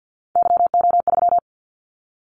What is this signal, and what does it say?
Morse code, heard as a keyed tone, about 50 wpm, 700 Hz. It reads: YO3
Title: Callsign Prefixes for Top 25 DXCC - 50wpm